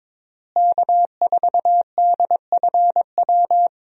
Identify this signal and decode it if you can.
Morse code, heard as a keyed tone, about 22 wpm, 700 Hz. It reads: K4DFW
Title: USA Call Signs - 5 Characters - Repeated 2x - 22wpm